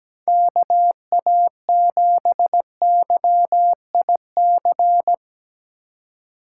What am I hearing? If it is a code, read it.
Morse code, heard as a keyed tone, about 18 wpm, 700 Hz. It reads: KA7YIC
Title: Speed Racing - USA 2x3 Call Signs 17wpm